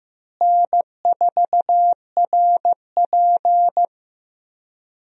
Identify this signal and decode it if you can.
Morse code, heard as a keyed tone, about 15 wpm, 700 Hz. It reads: N4RP
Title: USA Call Signs - 4 Characters - Repeated 3x - 15wpm